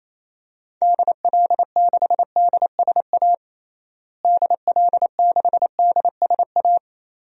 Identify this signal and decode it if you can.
Morse code, heard as a keyed tone, about 28 wpm, 700 Hz. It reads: DL6DSA DL6DSA
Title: DXCC-100 Call Signs - 6 Characters - Repeated 2x - 28wpm